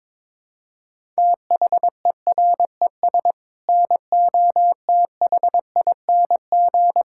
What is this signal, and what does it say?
Morse code, heard as a keyed tone, about 22 wpm, 700 Hz. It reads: THERES NOTHING
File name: practice / 22wpm / Sets of 7 Words - 22wpm